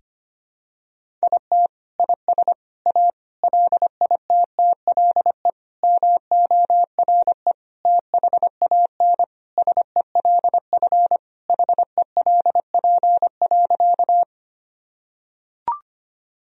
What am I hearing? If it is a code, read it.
Morse code, heard as a keyed tone, about 25 wpm, 700 Hz. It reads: IT IS A LITTLE MORE THAN SELF HELP.
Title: Sentences from Top 1000 Words - Review Part 1 - 25wpm